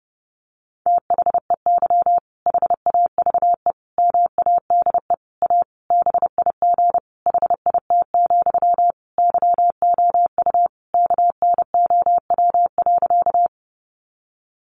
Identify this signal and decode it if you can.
Morse code, heard as a keyed tone, about 30 wpm, 700 Hz. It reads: THEY HAVE MADE A BIG HIT, YOU KNOW.
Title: Sentences from Top 1200 Words 30wpm